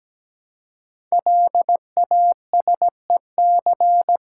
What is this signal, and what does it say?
Morse code, heard as a keyed tone, about 17 wpm, 700 Hz. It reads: LASEC
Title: Code Groups - 5 Letters - 17wpm